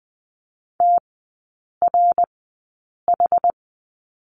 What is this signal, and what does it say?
Morse code, heard as a keyed tone, about 20 wpm, spaced as for 8 wpm, 700 Hz. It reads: TRH
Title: 060 Lesson 11e - Letter h - Three Characters 20wpm